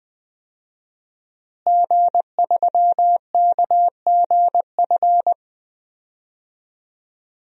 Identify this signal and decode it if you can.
Morse code, heard as a keyed tone, about 20 wpm, 700 Hz. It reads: G3KGF
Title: DXCC-100 Call Signs - 5 Characters - Repeated 2x - 20wpm